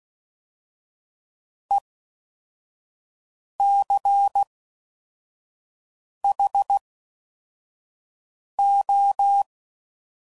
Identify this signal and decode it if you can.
Morse code, heard as a keyed tone, about 16 wpm, 800 Hz. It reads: E C H O
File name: Suffixes / Special INGs / Fast INGs